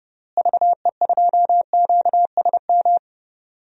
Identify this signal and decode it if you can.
Morse code, heard as a keyed tone, about 30 wpm, 700 Hz. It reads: VE2QSM
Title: DXCC-100 Call Signs - 6 Characters - Repeated 2x - 30wpm